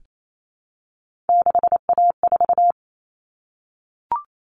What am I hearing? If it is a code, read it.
Morse code, heard as a keyed tone, about 28 wpm, 700 Hz. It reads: 6A4
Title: Common Cut Numbers - 3-Digit 28wpm